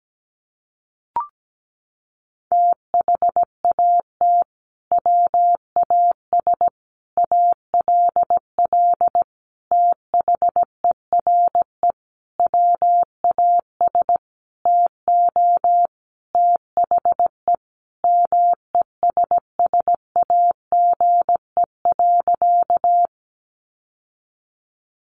Morse code, heard as a keyed tone, about 17 wpm, 700 Hz. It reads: THAT WAS ALL THERE WAS TO THE MESSAGE.
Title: Sentences from Top 1100 Words 17wpm